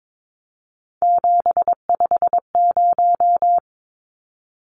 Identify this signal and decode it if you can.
Morse code, heard as a keyed tone, about 22 wpm, 700 Hz. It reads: 750
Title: Numbers - 3-Digit 22wpm